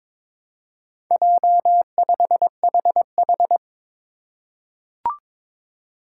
Morse code, heard as a keyed tone, about 21 wpm, 700 Hz. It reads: J5HH
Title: Four Challenging Letter-Number Combinations 22wpm